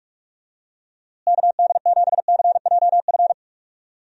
Morse code, heard as a keyed tone, about 46 wpm, 700 Hz. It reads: KD7CJF
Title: USA Call Signs - 6 Characters - Repeated 3x - 45wpm